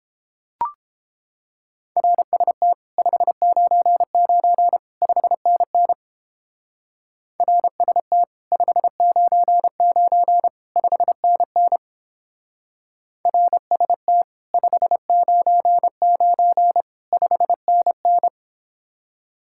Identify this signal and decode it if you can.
Morse code, heard as a keyed tone, about 29 wpm, 700 Hz. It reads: RST 599 5NN RST 599 5NN RST 599 5NN